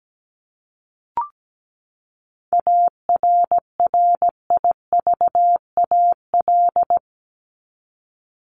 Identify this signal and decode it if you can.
Morse code, heard as a keyed tone, about 17 wpm, 700 Hz. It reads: ARRIVAL
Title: Top 1751-2000 Words 17wpm